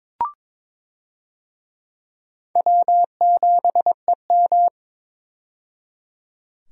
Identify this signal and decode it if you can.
Morse code, heard as a keyed tone, about 22 wpm, 700 Hz. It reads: W7EM
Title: USA Call Signs - 4 Characters - 22wpm